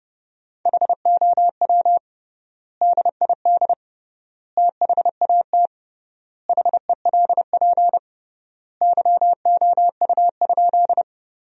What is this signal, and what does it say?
Morse code, heard as a keyed tone, about 32 wpm, 700 Hz. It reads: HOW DID THAT HELP YOU?
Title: 3x Word-Spacing - Sentences from Top 200 Words - 30wpm